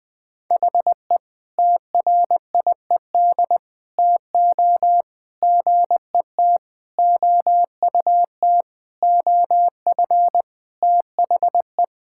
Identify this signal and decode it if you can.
Morse code, heard as a keyed tone, about 20 wpm, 700 Hz. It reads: HE TRIED TO GET OUT OF THE
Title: Sentences with 5-Word Phrases - Spoken Beginning - 20wpm